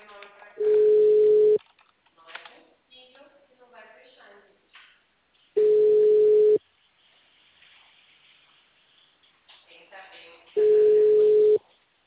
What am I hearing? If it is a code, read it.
Ringback tone (Europe, 425 Hz), with speech.